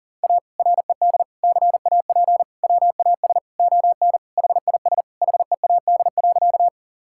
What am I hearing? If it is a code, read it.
Morse code, heard as a keyed tone, about 40 wpm, 700 Hz. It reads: A RED CAP WAS ON HIS HEAD.
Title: Sentences from Top 1600 Words 40wpm